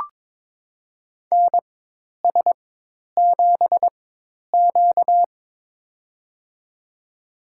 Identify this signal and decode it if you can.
Morse code, heard as a keyed tone, about 23 wpm, spaced as for 10 wpm, 700 Hz. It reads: NS7Q
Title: USA Call Signs - 4 Characters - 4x Character Spacing - 22wpm